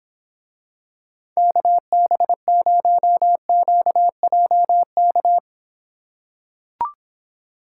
Morse code, heard as a keyed tone, about 26 wpm, 700 Hz. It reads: KB0QJK